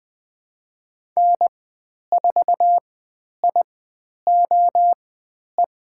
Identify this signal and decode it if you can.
Morse code, heard as a keyed tone, about 20 wpm, spaced as for 10 wpm, 700 Hz. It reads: N4IOE
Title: 095 Lesson 16g - Letter c - Call Signs 20wpm